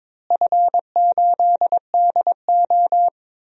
Morse code, heard as a keyed tone, about 22 wpm, 700 Hz. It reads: F8DO
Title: DXCC-100 Call Signs - 4 Characters - Repeated 3x - 22wpm